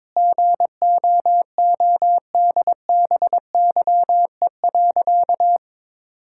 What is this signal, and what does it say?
Morse code, heard as a keyed tone, about 22 wpm, 700 Hz. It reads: GOODBYE.
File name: Sentences with 7-Word Phrases - 22wpm